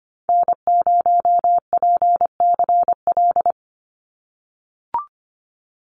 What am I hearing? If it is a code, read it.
Morse code, heard as a keyed tone, about 25 wpm, 690 Hz. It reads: N0PCL
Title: NA SOTA Activator Callsigns 25wpm